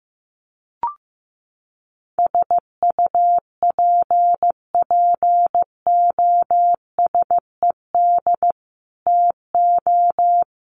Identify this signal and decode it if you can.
Morse code, heard as a keyed tone, about 15 wpm, 700 Hz. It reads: SUPPOSED TO
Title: Sets of 3 Words 15wpm